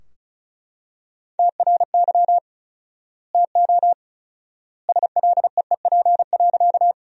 Morse code, heard as a keyed tone, about 36 wpm, 700 Hz. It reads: TRY TO SLEEP.
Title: 4x Word-Spacing - Sentences from Top 500 Words - 35wpm